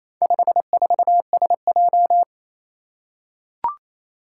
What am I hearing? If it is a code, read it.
Morse code, heard as a keyed tone, about 27 wpm, 700 Hz. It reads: 54SJ
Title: Four Challenging Letter-Number Combinations 28wpm